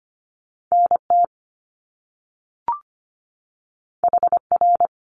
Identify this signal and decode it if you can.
Morse code, heard as a keyed tone, about 25 wpm, 700 Hz. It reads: NT HR
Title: Top Two Letter Combinations in Words 25wpm